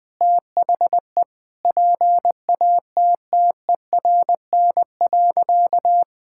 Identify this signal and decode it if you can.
Morse code, heard as a keyed tone, about 20 wpm, 700 Hz. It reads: THE PATTERN.
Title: Sentences with 5-Word Phrases - 20wpm